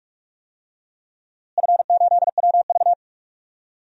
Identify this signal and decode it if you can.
Morse code, heard as a keyed tone, about 46 wpm, 700 Hz. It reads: F8WV
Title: DXCC-100 Call Signs - 4 Characters - Repeated 2x - 45wpm